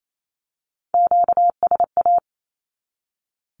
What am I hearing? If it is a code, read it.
Morse code, heard as a keyed tone, about 28 wpm, 700 Hz. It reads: QSA
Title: Q-Codes 28wpm